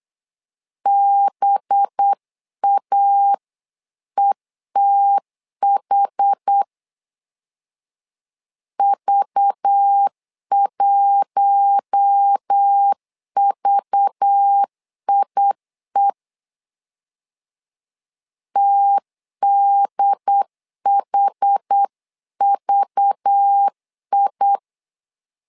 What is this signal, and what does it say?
Morse code, heard as a keyed tone, about 8 wpm, 800 Hz. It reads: BAETH V1VIE TDHVI
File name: LEZIONI / 5 LBVC / 5 LBV 8wpm 2837